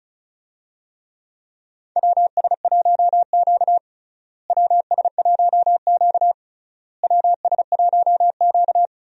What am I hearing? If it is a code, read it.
Morse code, heard as a keyed tone, about 36 wpm, 700 Hz. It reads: WS1Q WS1Q WS1Q
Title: USA Call Signs - 4 Characters - Repeated 3x - 35wpm